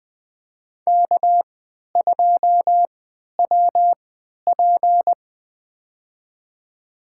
Morse code, heard as a keyed tone, about 20 wpm, spaced as for 11 wpm, 700 Hz. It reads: K2WP
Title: USA Call Signs - 4 Characters - 3x Character Spacing - 20wpm